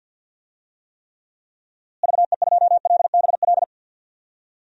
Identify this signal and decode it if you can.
Morse code, heard as a keyed tone, about 50 wpm, 700 Hz. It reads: VE2LBL
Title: DXCC-100 Call Signs - 6 Characters - Repeated 3x - 50wpm